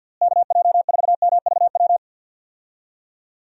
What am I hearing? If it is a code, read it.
Morse code, heard as a keyed tone, about 52 wpm, 700 Hz. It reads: KJ4MVW
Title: USA Call Signs - 6 Characters - Repeated 3x - 50wpm